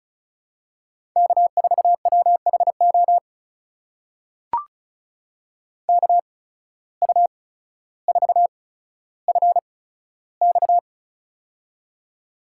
Morse code, heard as a keyed tone, about 36 wpm, spaced as for 15 wpm, 700 Hz. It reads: K4WHO KU4FX
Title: USA Call Signs - 5 Characters - 8x Character Spacing - 35wpm